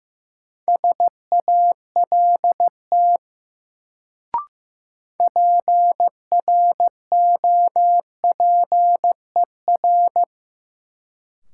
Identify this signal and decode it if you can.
Morse code, heard as a keyed tone, about 15 wpm, 700 Hz. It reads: SALT PROPER
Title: Sentences from Top 800 Words - New Word Drill - 15wpm